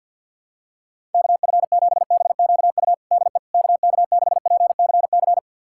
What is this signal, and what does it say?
Morse code, heard as a keyed tone, about 51 wpm, 700 Hz. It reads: KF7BQU DE KK6PCC
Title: USA Callsign DE Callsign 50wpm